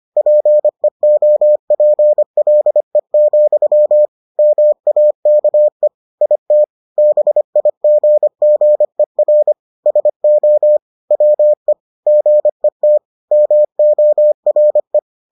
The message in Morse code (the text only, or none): PEOPLE, MAKE IT BIGGER SO WE GET MORE